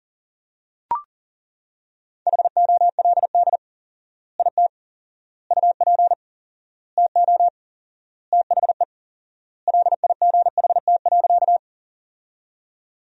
Morse code, heard as a keyed tone, about 37 wpm, 700 Hz. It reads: HOLD IT UP TO THE LIGHT.
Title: 4x Word-Spacing - Sentences from Top 400 Words - 40wpm